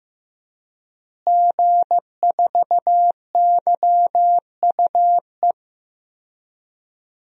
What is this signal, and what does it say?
Morse code, heard as a keyed tone, about 15 wpm, 700 Hz. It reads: G4YUE
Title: DXCC-100 Call Signs - 5 Characters - Repeated 2x - 15wpm